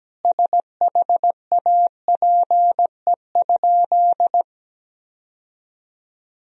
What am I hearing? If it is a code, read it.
Morse code, heard as a keyed tone, about 17 wpm, 700 Hz. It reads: SHAPE?
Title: Sentences from Top 1000 Words - Review Part 3 - 17wpm